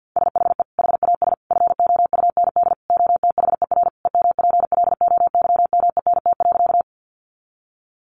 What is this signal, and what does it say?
Morse code, heard as a keyed tone, about 50 wpm, 700 Hz. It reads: SHE HAS FOUND OTHER EMPLOYMENT.
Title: Sentences from Top 1500 Words 50wpm